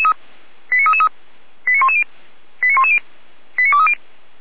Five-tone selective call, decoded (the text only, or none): 89333 89110 89110 89230